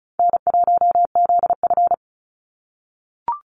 Morse code, heard as a keyed tone, about 34 wpm, 700 Hz. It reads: N1ZF
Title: NA SOTA Activator Callsigns 35wpm